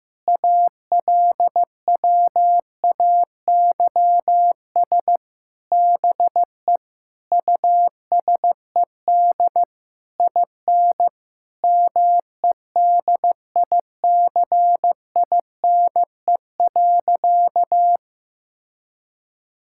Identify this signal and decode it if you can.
Morse code, heard as a keyed tone, about 16 wpm, 700 Hz. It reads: ALWAYS BE USED IN MEDICINE.
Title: Sentences from Top 1000 Words - Review Part 6 - 15wpm